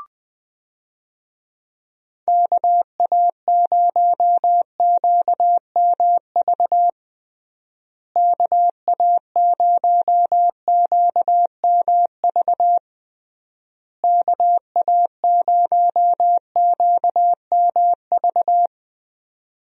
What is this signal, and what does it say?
Morse code, heard as a keyed tone, about 21 wpm, 700 Hz. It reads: KA0QMV KA0QMV KA0QMV
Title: USA Call Signs - 6 Characters - Repeated 3x - 20wpm